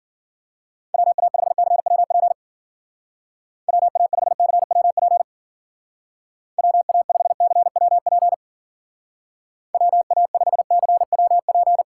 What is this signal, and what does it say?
Morse code, heard as a keyed tone, about 51 wpm, 700 Hz. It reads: WA5CWP WA5CWP WA5CWP WA5CWP